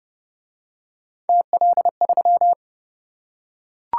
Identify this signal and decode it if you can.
Morse code, heard as a keyed tone, about 27 wpm, 700 Hz. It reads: TL3